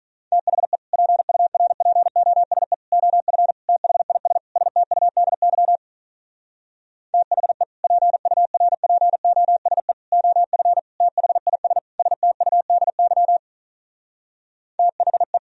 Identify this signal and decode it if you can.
Morse code, heard as a keyed tone, about 45 wpm, 700 Hz. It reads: THE PURPOSE OF THIS STUDY THE PURPOSE OF THIS STUDY THE